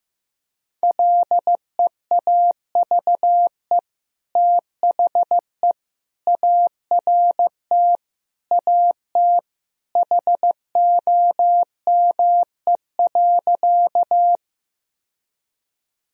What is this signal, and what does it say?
Morse code, heard as a keyed tone, about 15 wpm, 700 Hz. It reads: LEAVE THE ART AT HOME.
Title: Sentences from Top 1000 Words - Review Part 3 - 15wpm